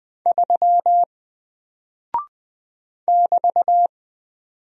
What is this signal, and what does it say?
Morse code, heard as a keyed tone, about 20 wpm, 700 Hz. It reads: 3 <BT>
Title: Single Character (Letter:Number:Punctuation:Prosign) Alt 20wpm